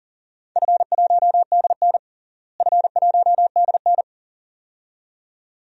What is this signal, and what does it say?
Morse code, heard as a keyed tone, about 40 wpm, 700 Hz. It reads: F1DN F1DN
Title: DXCC-100 Call Signs - 4 Characters - Repeated 2x - 40wpm